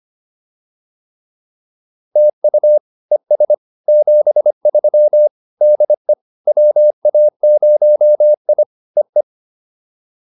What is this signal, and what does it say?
Morse code, heard as a keyed tone, about 25 wpm, 590 Hz. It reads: TU ES 73 DE WA0I EE